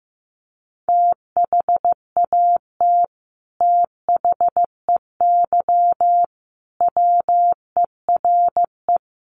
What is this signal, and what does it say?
Morse code, heard as a keyed tone, about 15 wpm, 700 Hz. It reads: THAT THEY WERE
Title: Sets of 5 Words Encore 15wpm